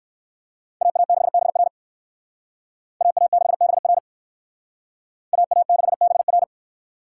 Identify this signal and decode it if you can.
Morse code, heard as a keyed tone, about 58 wpm, 700 Hz. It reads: AA6BR AA6BR AA6BR